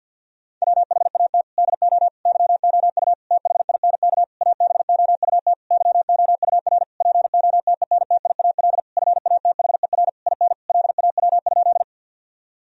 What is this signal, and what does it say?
Morse code, heard as a keyed tone, about 50 wpm, 700 Hz. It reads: WHAT DO YOU THINK ABOUT YOUR POTENTIAL FATHER IN LAW?
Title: Sentences from Top 1000 Words - Review Part 4 - 50wpm